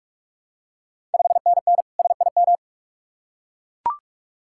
Morse code, heard as a keyed tone, about 44 wpm, 690 Hz. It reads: 5NN SIM
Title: Ontario QSO Party Exchange 45wpm